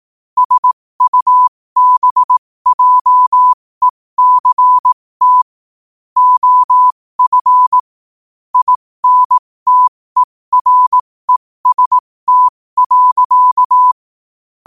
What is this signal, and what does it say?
Morse code, heard as a keyed tone, about 18 wpm, spaced as for 15 wpm, 1000 Hz. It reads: SUBJECT OF INTEREST.
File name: qotdfiles / 15wpm